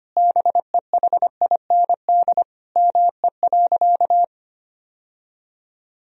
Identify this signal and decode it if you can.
Morse code, heard as a keyed tone, about 25 wpm, 700 Hz. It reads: BEHIND ME.
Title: Sentences from Top 1000 Words - Review Part 1 - 25wpm